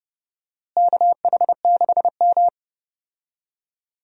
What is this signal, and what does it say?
Morse code, heard as a keyed tone, about 30 wpm, 700 Hz. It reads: KH6M
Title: USA Call Signs - 4 Characters - Repeated 2x - 30wpm